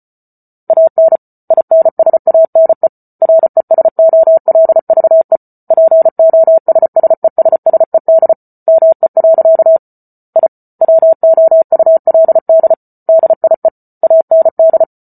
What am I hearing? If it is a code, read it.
Morse code, heard as a keyed tone, about 34 wpm, 650 Hz. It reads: AN INSANE RESOLVE POSSESSED ME. I WOULD DIE AND